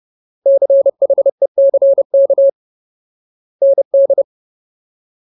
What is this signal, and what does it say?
Morse code, heard as a keyed tone, about 30 wpm, 550 Hz. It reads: CHECK ND
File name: Ultimate Word List / 30WPM Ultimate